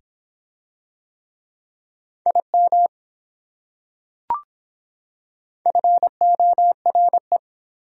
Morse code, heard as a keyed tone, about 25 wpm, 700 Hz. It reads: IM FORE